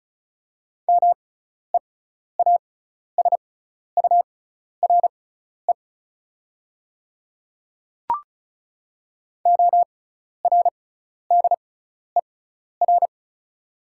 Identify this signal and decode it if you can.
Morse code, heard as a keyed tone, about 36 wpm, spaced as for 12 wpm, 690 Hz. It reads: MEASURE ORDER
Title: 6x Character Spacing - Top 500 Words - 35wpm